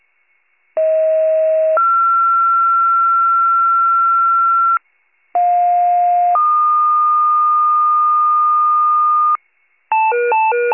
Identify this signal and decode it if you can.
Two-tone page, heard as a one-second tone, then a three-second tone, about 640 then 1360 Hz, about 710 then 1220 Hz, with noise. It is heard from 0.8 to 9.4 s.
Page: County Wide Paging Archive